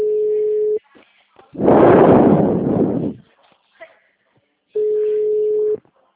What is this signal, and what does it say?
Ringback tone (Europe, 425 Hz), with speech.